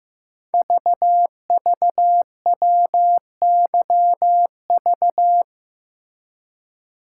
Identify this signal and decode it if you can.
Morse code, heard as a keyed tone, about 15 wpm, 700 Hz. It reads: VVWYV
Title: Code Groups - 5 Letters - 15wpm